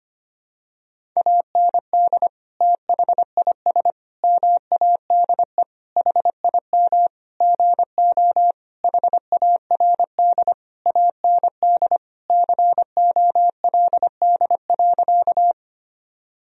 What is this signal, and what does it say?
Morse code, heard as a keyed tone, about 25 wpm, 700 Hz. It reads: AND THIS MADE HIM GO HARD AND COLD.
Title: Sentences from Top 1000 Words - Review Part 1 - 25wpm